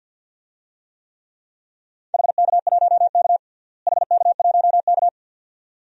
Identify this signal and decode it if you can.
Morse code, heard as a keyed tone, about 51 wpm, 700 Hz. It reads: HK1K HK1K
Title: DXCC-100 Call Signs - 4 Characters - Repeated 2x - 50wpm